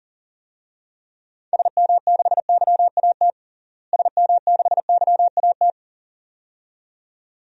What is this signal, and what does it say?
Morse code, heard as a keyed tone, about 40 wpm, 700 Hz. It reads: SM6YAT SM6YAT
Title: DXCC-100 Call Signs - 6 Characters - Repeated 2x - 40wpm